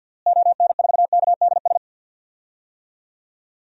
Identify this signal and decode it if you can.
Morse code, heard as a keyed tone, about 51 wpm, 700 Hz. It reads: ON4KDS
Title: DXCC-100 Call Signs - 6 Characters - Repeated 3x - 50wpm